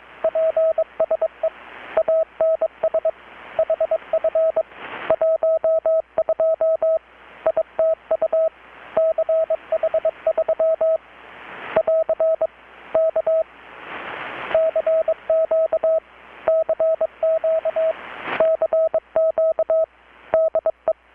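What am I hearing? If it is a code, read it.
Morse code, heard as a keyed tone, about 21 wpm, 640 Hz. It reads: PSE ANS HF 12 ITU CH3 <AR> K CQ CQ CQ DE
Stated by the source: Morse from KPH's Night of Nights XX maritime radiotelegraphy event, received in Chiapas, Mexico about 3,800 km away on a Kenwood R-600 with a grounded 10 m long-wire antenna.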